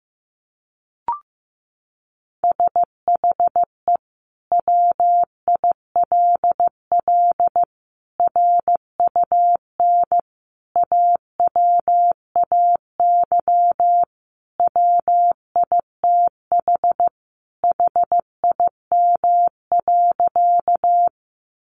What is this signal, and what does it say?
Morse code, heard as a keyed tone, about 15 wpm, 700 Hz. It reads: SHE WILL RUN AWAY WITH HIM.
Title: Sentences from Top 600 Words 15wpm